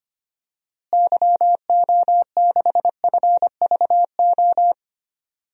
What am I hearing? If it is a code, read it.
Morse code, heard as a keyed tone, about 25 wpm, 700 Hz. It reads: YO6FVO
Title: DXCC-100 Call Signs - 6 Characters - Repeated 3x - 25wpm